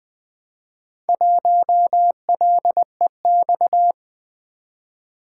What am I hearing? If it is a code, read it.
Morse code, heard as a keyed tone, about 20 wpm, 700 Hz. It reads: 1LEX
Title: Four Letter-Number Combinations - 20wpm